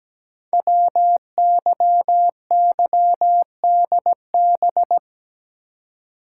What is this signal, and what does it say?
Morse code, heard as a keyed tone, about 17 wpm, 700 Hz. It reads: WYYDB
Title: Code Groups - 5 Letters - 17wpm